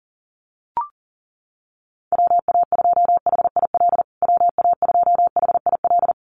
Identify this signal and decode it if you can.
Morse code, heard as a keyed tone, about 40 wpm, 700 Hz. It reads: WA2HIL WA2HIL
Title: USA Call Signs Repeated 3x 40wpm